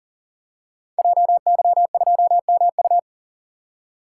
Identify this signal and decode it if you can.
Morse code, heard as a keyed tone, about 38 wpm, 700 Hz. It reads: JY2MU
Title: Code Groups - 5 Letter:Numbers - 40wpm